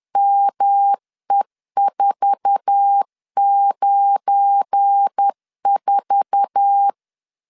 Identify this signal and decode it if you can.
Morse code, heard as a keyed tone, about 10 wpm, 800 Hz. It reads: ME494
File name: LEZIONI / 8 YJ45 / 8 YJ4 10wpm 2633